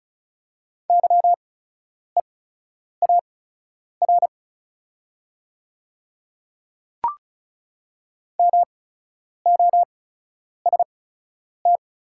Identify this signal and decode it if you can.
Morse code, heard as a keyed tone, about 36 wpm, spaced as for 10 wpm, 690 Hz. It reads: YEAR MOST
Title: 8x Character Spacing - Top 300 Words - 35wpm